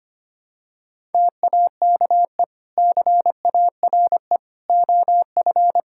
Morse code, heard as a keyed tone, about 25 wpm, 700 Hz. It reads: TAKE CARE OF
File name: practice / 25wpm / Sentences with 2-Word Phrases - 25wpm